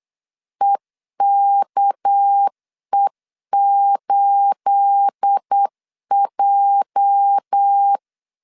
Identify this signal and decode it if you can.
Morse code, heard as a keyed tone, about 8 wpm, 800 Hz. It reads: EKE8J
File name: CW / DET / 1 9 1500C S A 8wpm 5637